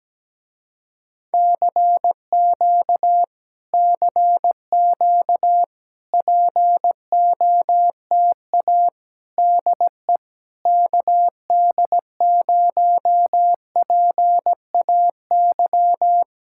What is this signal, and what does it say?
Morse code, heard as a keyed tone, about 17 wpm, 700 Hz. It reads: CQ CQ POTA DE KD0PAY
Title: Speed-Racing Common QSO Phrases - 17wpm